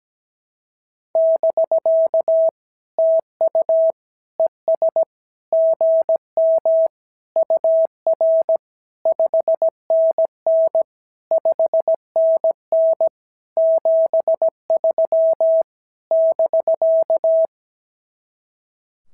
Morse code, heard as a keyed tone, about 16 wpm, 650 Hz. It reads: <BK> TU ES GM UR 5NN 5NN 73 <BK>
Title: SOTA QSO Practice 17wpm